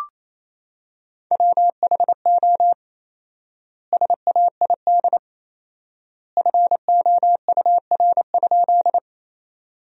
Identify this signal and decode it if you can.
Morse code, heard as a keyed tone, about 28 wpm, 700 Hz. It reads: WHO SAID FOUR?
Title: 4x Word-Spacing - Sentences from Top 300 Words - 28wpm